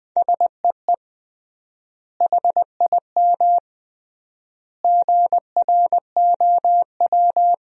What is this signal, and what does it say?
Morse code, heard as a keyed tone, about 20 wpm, 700 Hz. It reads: SEE HIM GROW
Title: 3x Word-Spacing - Sentences from Top 300 Words - 20wpm